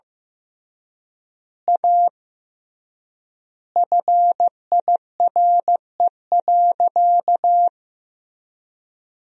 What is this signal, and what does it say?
Morse code, heard as a keyed tone, about 15 wpm, 700 Hz. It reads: A FIRE.
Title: 3x Word-Spacing - Sentences from Top 400 Words - 15wpm